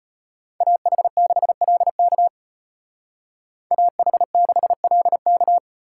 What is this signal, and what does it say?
Morse code, heard as a keyed tone, about 33 wpm, 700 Hz. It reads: AH6LK AH6LK